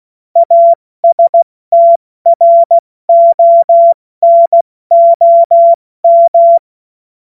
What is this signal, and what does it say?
Morse code, heard as a keyed tone, about 16 wpm, spaced as for 14 wpm, 680 Hz. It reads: ASTRONOM